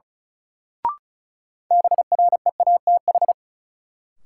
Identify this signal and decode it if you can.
Morse code, heard as a keyed tone, about 36 wpm, 700 Hz. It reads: BREATH
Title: Sets of 5 Words Encore - New Word Drill - 35wpm